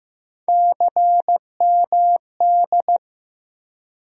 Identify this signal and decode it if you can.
Morse code, heard as a keyed tone, about 15 wpm, 700 Hz. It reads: CMD
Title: USA Call Sign Postfixes - 3 Letters 15wpm